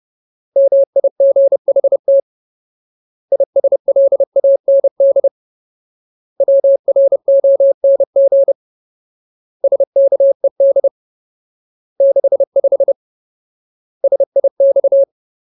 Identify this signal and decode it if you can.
Morse code, heard as a keyed tone, about 30 wpm, 550 Hz. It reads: MIGHT ISLAND WRONG SKED 65 SIX
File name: Ultimate Word List / 30WPM Ultimate